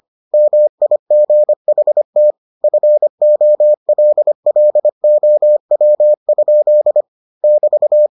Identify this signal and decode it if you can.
Morse code, heard as a keyed tone, about 25 wpm, 600 Hz. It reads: MIGHT FOLLOW? <BT>